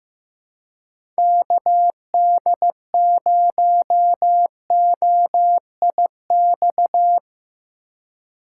Morse code, heard as a keyed tone, about 15 wpm, 700 Hz. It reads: KD0OIX
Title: K1USN SST QSO Practice - 15wpm